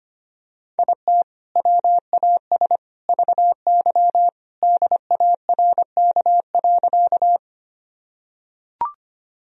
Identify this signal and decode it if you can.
Morse code, heard as a keyed tone, about 25 wpm, 700 Hz. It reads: IT WAS VY DARK.
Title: Sentences from Top 500 Words and Abbreviations 25wpm